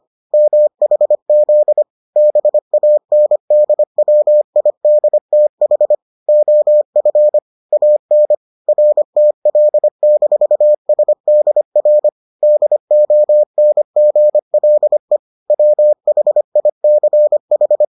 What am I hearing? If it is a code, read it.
Morse code, heard as a keyed tone, about 25 wpm, 600 Hz. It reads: MHZ BANDWIDTH OF AN RTL-SDR DONGLE WHICH